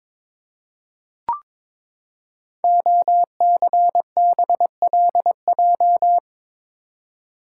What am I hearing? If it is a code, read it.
Morse code, heard as a keyed tone, about 22 wpm, 700 Hz. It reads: OCBLJ
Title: Code Groups - 5 Letters - 22wpm